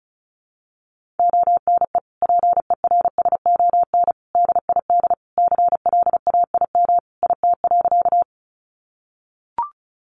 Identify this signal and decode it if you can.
Morse code, heard as a keyed tone, about 34 wpm, 700 Hz. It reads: ONE PERSON DID CLAIM IT.
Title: Sentences from Top 1500 Words 35wpm